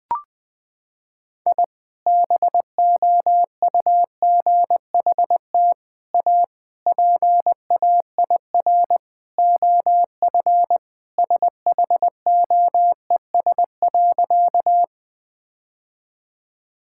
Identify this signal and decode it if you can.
Morse code, heard as a keyed tone, about 20 wpm, 700 Hz. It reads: I BOUGHT A PAIR OF SHOES.
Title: Sentences with 3-Word Phrases - 20wpm